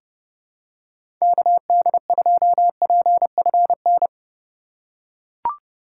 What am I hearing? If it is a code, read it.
Morse code, heard as a keyed tone, about 32 wpm, 700 Hz. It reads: KD2PFN
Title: USA Call Signs - 4x Character Spacing - 30wpm